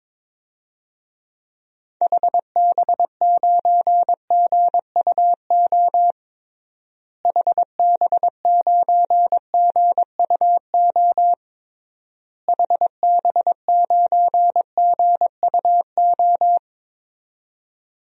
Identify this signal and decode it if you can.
Morse code, heard as a keyed tone, about 22 wpm, 700 Hz. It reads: HB9GUO HB9GUO HB9GUO
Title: DXCC-100 Call Signs - 6 Characters - Repeated 3x - 22wpm